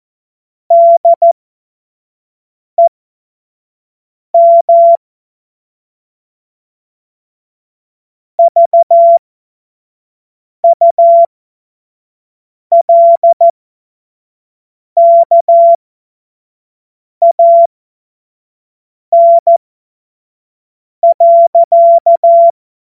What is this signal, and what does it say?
Morse code, heard as a keyed tone, about 14 wpm, spaced as for 5 wpm, 680 Hz. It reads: DEM VULKAN.
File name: Morsekurs / Witze / Witz 28. März 2023 25 F70